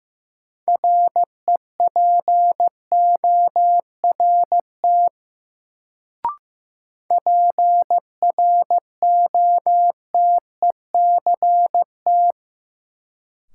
Morse code, heard as a keyed tone, about 15 wpm, 700 Hz. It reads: REPORT PROTECT
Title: Sentences from Top 900 Words - New Word Drill - 15wpm